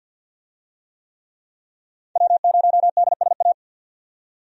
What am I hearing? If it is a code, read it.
Morse code, heard as a keyed tone, about 51 wpm, 700 Hz. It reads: W0DSA
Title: USA Call Signs - 5 Characters - 50wpm